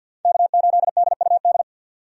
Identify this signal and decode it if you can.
Morse code, heard as a keyed tone, about 51 wpm, 700 Hz. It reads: K8DUD
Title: USA Call Signs - 5 Characters - 50wpm